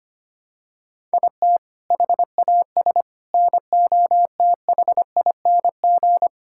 Morse code, heard as a keyed tone, about 25 wpm, 700 Hz. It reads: IT HAS NOTHING
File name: practice / 25wpm / Sentences with 4-Word Phrases - 25wpm